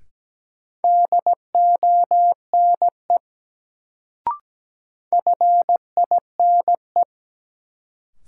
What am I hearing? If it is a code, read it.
Morse code, heard as a keyed tone, about 17 wpm, 700 Hz. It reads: DONE FINE
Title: Sentences from Top 500 Words - New Word Drill - 17wpm